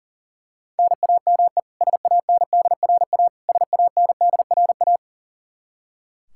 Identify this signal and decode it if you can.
Morse code, heard as a keyed tone, about 40 wpm, 700 Hz. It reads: NAME SANDRA SANDRA
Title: Practice QSO - Exchange 1 - Beginning - 40wpm